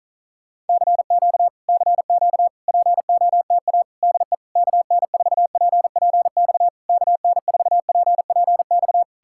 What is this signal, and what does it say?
Morse code, heard as a keyed tone, about 48 wpm, 700 Hz. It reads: CQ CQ POTA DE KN4PPX KN4PPX